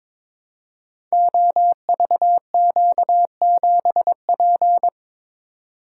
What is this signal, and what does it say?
Morse code, heard as a keyed tone, about 22 wpm, 700 Hz. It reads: OVQ7P
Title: Code Groups - 5 Letter:Numbers - 22wpm